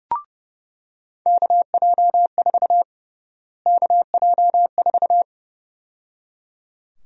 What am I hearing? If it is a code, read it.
Morse code, heard as a keyed tone, about 30 wpm, 700 Hz. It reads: KJ4 KJ4
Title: USA Call Sign Prefixes - Repeated 2x - 30wpm